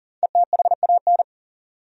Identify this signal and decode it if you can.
Morse code, heard as a keyed tone, about 40 wpm, 700 Hz. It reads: ETHAN
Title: USA Common Names - Max 5 Letters - 40wpm